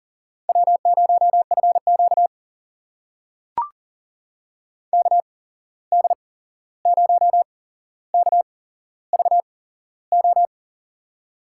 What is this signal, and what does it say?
Morse code, heard as a keyed tone, about 39 wpm, spaced as for 15 wpm, 700 Hz. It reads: W0FQ KD0KVO
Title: USA Call Signs - 8x Character Spacing - 40wpm